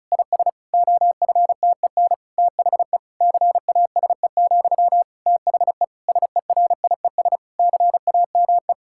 Morse code, heard as a keyed tone, about 36 wpm, 700 Hz. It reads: IS OFTEN THE CASE, THE SERIES CAME
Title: Sentences from Top 1000 Words - Review Part 5 - 35wpm